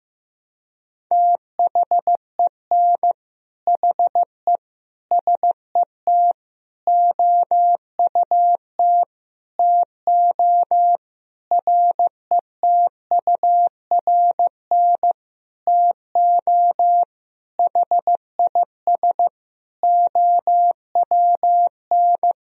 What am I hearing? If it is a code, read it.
Morse code, heard as a keyed tone, about 15 wpm, 700 Hz. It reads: THEN HE SET OUT TO RETURN TO HIS OWN
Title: Sentences from Top 1000 Words - Review Part 1 - 15wpm